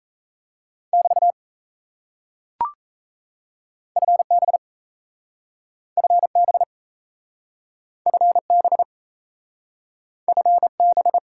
Morse code, heard as a keyed tone, about 37 wpm, 700 Hz. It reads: <BT> FB FB FB FB